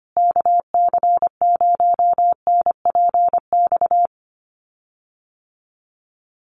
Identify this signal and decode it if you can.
Morse code, heard as a keyed tone, about 25 wpm, 700 Hz. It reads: KC0NPX
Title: USA Call Signs Repeated 2x 25wpm